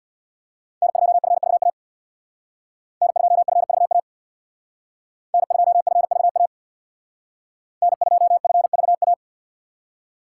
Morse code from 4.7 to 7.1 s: N2FVA

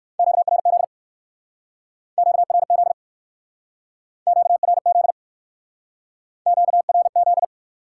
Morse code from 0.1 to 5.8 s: QRZ QRZ QRZ

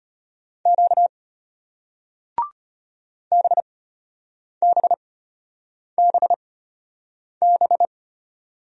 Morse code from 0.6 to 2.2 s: Q

